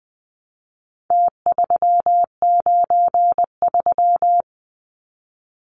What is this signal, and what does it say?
Morse code, heard as a keyed tone, about 20 wpm, 700 Hz. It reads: T393
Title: Numbers - Contest Style - 4-Digit 20wpm